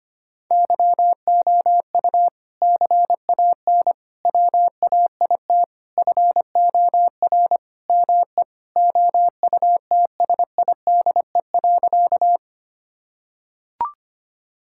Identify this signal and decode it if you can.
Morse code, heard as a keyed tone, about 25 wpm, 700 Hz. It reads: YOU CAN WAIT FOR ME OUTSIDE.
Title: Sentences from Top 1000 Words - Review Part 6 - 25wpm